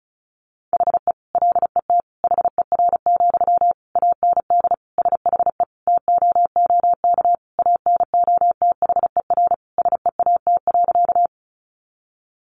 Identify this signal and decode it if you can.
Morse code, heard as a keyed tone, about 34 wpm, 700 Hz. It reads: HE LET HER, AND SHE TOOK ANOTHER SEAT.
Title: Sentences from Top 1500 Words 35wpm